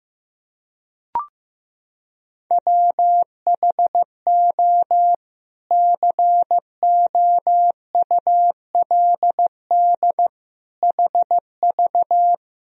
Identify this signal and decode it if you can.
Morse code, heard as a keyed tone, about 16 wpm, 700 Hz. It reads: WHO COULD HV
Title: Sentences from Top 300 Words and Abbreviations 15wpm